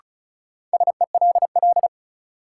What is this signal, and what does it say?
Morse code, heard as a keyed tone, about 36 wpm, 700 Hz. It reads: SELL
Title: Sentences from Top 1000 Words - Review Part 6 - New Word Drill - 35wpm